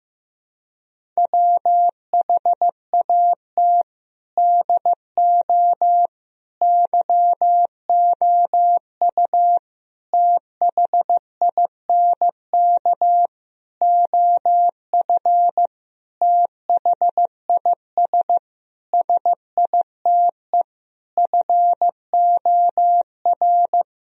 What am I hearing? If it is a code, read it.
Morse code, heard as a keyed tone, about 15 wpm, 700 Hz. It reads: WHAT DO YOU THINK OF THIS SITE FOR
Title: Sentences from Top 1000 Words - Review Part 1 - 15wpm